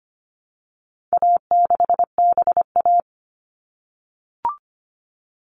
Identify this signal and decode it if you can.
Morse code, heard as a keyed tone, about 25 wpm, 700 Hz. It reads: A6BA
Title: All Cut Numbers - 4-Digit 25wpm